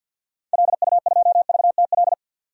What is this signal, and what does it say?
Morse code, heard as a keyed tone, about 52 wpm, 700 Hz. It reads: LU2VTL
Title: DXCC-100 Call Signs - 6 Characters - 50wpm